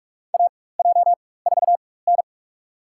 Morse code, heard as a keyed tone, about 46 wpm, spaced as for 21 wpm, 700 Hz. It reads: AJ4N
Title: USA Call Signs - 4x Character Spacing - 45wpm